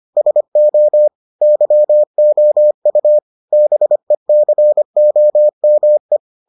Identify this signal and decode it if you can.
Morse code, heard as a keyed tone, about 25 wpm, 600 Hz. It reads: SO YOU BECOME